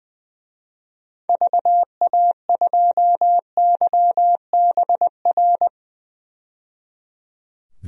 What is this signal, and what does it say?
Morse code, heard as a keyed tone, about 20 wpm, 700 Hz. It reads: VA2YBR
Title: DXCC-100 Call Signs - 6 Characters - 20wpm